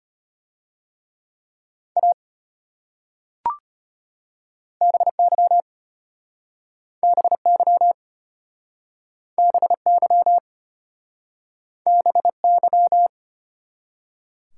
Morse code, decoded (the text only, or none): A BY BY BY BY